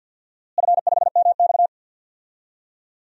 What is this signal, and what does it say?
Morse code, heard as a keyed tone, about 52 wpm, 700 Hz. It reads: F5MX
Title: DXCC-100 Call Signs - 4 Characters - Repeated 3x - 50wpm